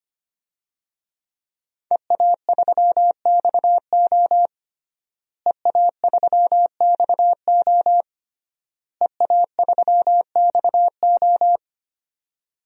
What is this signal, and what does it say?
Morse code, heard as a keyed tone, about 25 wpm, 700 Hz. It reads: EA3XO EA3XO EA3XO
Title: DXCC-100 Call Signs - 5 Characters - Repeated 3x - 25wpm